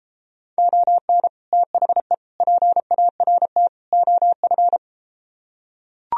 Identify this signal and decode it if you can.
Morse code, heard as a keyed tone, about 33 wpm, 700 Hz. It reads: ON THE PART OF